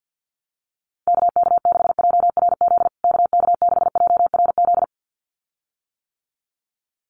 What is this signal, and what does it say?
Morse code, heard as a keyed tone, about 50 wpm, 700 Hz. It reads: KK6JRZ KK6JRZ
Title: USA Call Signs Repeated 2x 50wpm